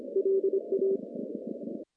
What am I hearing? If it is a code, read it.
Morse code, heard as a keyed tone, about 25 wpm, 390 Hz. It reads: LA
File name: contests / CW-SS-k5tr- -1400